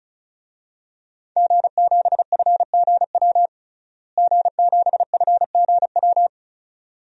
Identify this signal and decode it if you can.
Morse code, heard as a keyed tone, about 34 wpm, 700 Hz. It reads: G7FGW G7FGW
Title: DXCC-100 Call Signs - 5 Characters - Repeated 2x - 35wpm